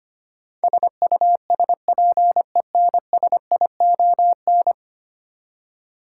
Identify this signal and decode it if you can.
Morse code, heard as a keyed tone, about 25 wpm, 700 Hz. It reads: SUSPENSION
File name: 10 Letter Words - First 3 Letters Given - 25wpm